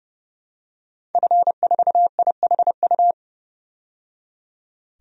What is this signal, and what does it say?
Morse code, heard as a keyed tone, about 32 wpm, 700 Hz. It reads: F4IHU
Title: DXCC-100 Call Signs - 5 Characters - 30wpm